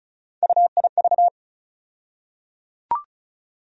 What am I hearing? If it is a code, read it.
Morse code, heard as a keyed tone, about 36 wpm, 700 Hz. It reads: UIV
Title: USA Call Sign Postfixes - 3 Letters 35wpm